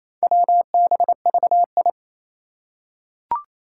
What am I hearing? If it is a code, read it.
Morse code, heard as a keyed tone, about 29 wpm, 700 Hz. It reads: WBVI
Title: Four Challenging Letter-Number Combinations 28wpm